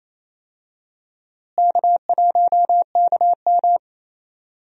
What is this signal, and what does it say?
Morse code, heard as a keyed tone, about 27 wpm, 700 Hz. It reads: K1KM
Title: USA Call Signs - 4 Characters - Repeated 2x - 28wpm